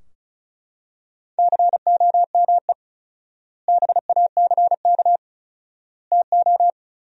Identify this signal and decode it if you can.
Morse code, heard as a keyed tone, about 34 wpm, 700 Hz. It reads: COME BACK TO
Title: 4x Word-Spacing - Sentences from Top 500 Words - 35wpm